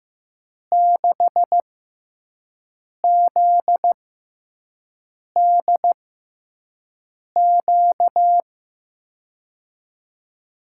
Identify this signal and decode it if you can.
Morse code, heard as a keyed tone, about 15 wpm, spaced as for 5 wpm, 700 Hz. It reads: 6ZDQ
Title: USA Call Signs - 6 Characters - 6x Character Spacing - 15wpm